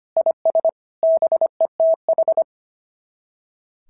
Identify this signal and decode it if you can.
Morse code, heard as a keyed tone, about 24 wpm, 650 Hz. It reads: IS BETH
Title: SOTA QSO Practice 25wpm